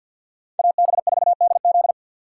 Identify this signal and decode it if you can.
Morse code, heard as a keyed tone, about 52 wpm, 700 Hz. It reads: AB4DZ
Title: USA Call Signs - 4x Character Spacing - 50wpm